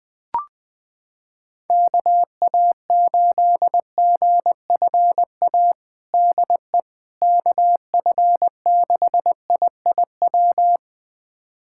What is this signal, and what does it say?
Morse code, heard as a keyed tone, about 21 wpm, 700 Hz. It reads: KA8GFA DE KF6IIW
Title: USA Callsign DE Callsign 20wpm